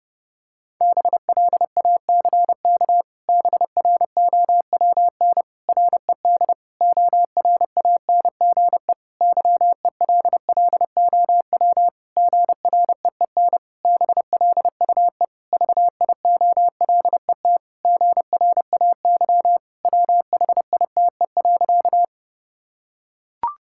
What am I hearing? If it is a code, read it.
Morse code, heard as a keyed tone, about 30 wpm, 700 Hz. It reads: BLACK BROWN RED ORANGE YELLOW GREEN BLUE VIOLET GRAY WHITE.
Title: Pangram Challenge v2 - 30wpm